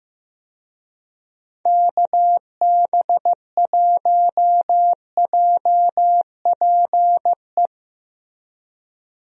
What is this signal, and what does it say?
Morse code, heard as a keyed tone, about 15 wpm, 700 Hz. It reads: KB1JPE
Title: USA Call Signs - 6 Characters - Repeated 3x - 15wpm